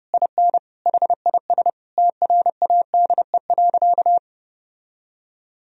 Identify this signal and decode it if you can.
Morse code, heard as a keyed tone, about 29 wpm, 700 Hz. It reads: IN HIS TRADE.
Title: Sentences from Top 1000 Words - Review Part 4 - 30wpm